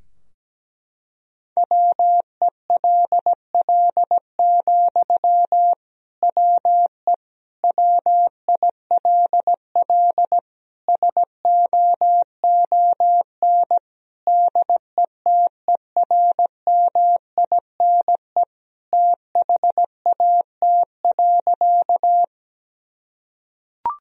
Morse code, heard as a keyed tone, about 17 wpm, 700 Hz. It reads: WELL, WE WILL SOON DETERMINE THAT.
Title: Sentences from Top 1000 Words - Review Part 4 - 17wpm